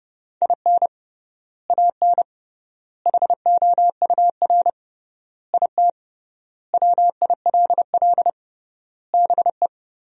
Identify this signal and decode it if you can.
Morse code, heard as a keyed tone, about 30 wpm, 700 Hz. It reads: IN AN HOUR IT WILL BE
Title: 3x Word-Spacing - Sentences from Top 400 Words - 30wpm